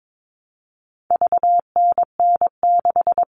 Morse code, heard as a keyed tone, about 22 wpm, 700 Hz. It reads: VNN6
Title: All Cut Numbers - 4-Digit 22wpm